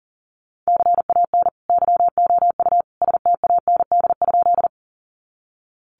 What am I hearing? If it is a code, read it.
Morse code, heard as a keyed tone, about 40 wpm, 700 Hz. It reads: CAN YOU STAND?
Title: Sentences from Top 300 Words 40wpm